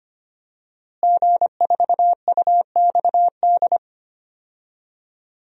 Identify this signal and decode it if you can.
Morse code, heard as a keyed tone, about 25 wpm, 700 Hz. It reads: G4UXD
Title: DXCC-100 Call Signs - 5 Characters - Repeated 2x - 25wpm